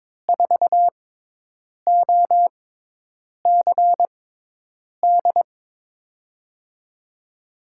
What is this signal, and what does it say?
Morse code, heard as a keyed tone, about 22 wpm, spaced as for 8 wpm, 700 Hz. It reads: 4OCD
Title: USA Call Signs - 6x Character Spacing - 22wpm